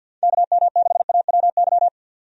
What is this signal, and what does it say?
Morse code, heard as a keyed tone, about 51 wpm, 700 Hz. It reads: KM6AWY
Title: USA Call Signs - 6 Characters - 50wpm